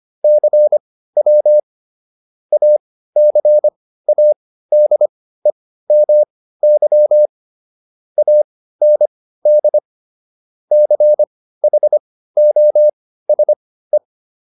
Morse code, heard as a keyed tone, about 25 wpm, spaced as for 15 wpm, 600 Hz. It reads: CW ACADEMY AND CHOSE